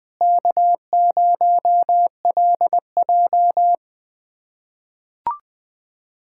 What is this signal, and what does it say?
Morse code, heard as a keyed tone, about 20 wpm, 700 Hz. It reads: K0LJ
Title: USA Callsign DE Callsign - 4 Characters - 20wpm